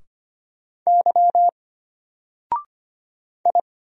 Morse code, heard as a keyed tone, about 25 wpm, 700 Hz. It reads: Y I
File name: Single-Letter-Number-25wpm